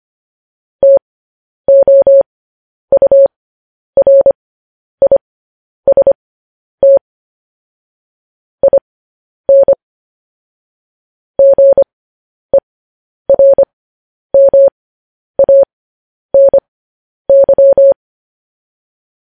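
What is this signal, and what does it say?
Morse code, heard as a keyed tone, about 25 wpm, spaced as for 10 wpm, 550 Hz. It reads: TOURIST IN GERMANY
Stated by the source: Morse code news headlines sent at 10 words per minute.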